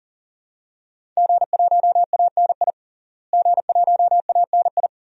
Morse code, heard as a keyed tone, about 40 wpm, 700 Hz. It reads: G1ANI G1ANI
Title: DXCC-100 Call Signs - 5 Characters - Repeated 2x - 40wpm